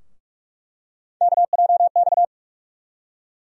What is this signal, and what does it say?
Morse code, heard as a keyed tone, about 46 wpm, 700 Hz. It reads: KJX
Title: USA Call Sign Postfixes - 3 Letters 45wpm